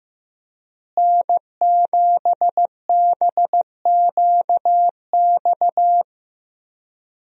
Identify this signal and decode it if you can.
Morse code, heard as a keyed tone, about 16 wpm, 700 Hz. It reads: N7BQX
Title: USA Call Signs - 5 Characters - Repeated 3x - 15wpm